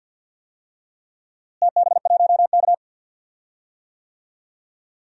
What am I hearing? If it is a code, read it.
Morse code, heard as a keyed tone, about 52 wpm, 700 Hz. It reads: TB1K
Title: DXCC-100 Call Signs - 4 Characters - Repeated 2x - 50wpm